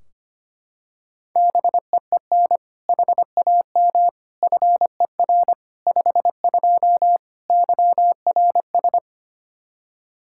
Morse code, heard as a keyed tone, about 25 wpm, 700 Hz. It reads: BEEN HAM FER 52 YRS
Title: Practice QSO - Exchange 3 - More Info - 25wpm